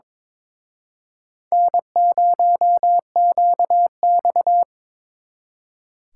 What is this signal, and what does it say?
Morse code, heard as a keyed tone, about 26 wpm, 700 Hz. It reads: N0QX
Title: Speed Racing - USA 1x2 Call Signs 22wpm